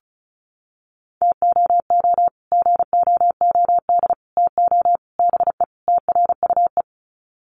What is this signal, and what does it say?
Morse code, heard as a keyed tone, about 34 wpm, 700 Hz. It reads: TOO GOOD TO BE TRUE
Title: Sets of 5 Words Encore 35wpm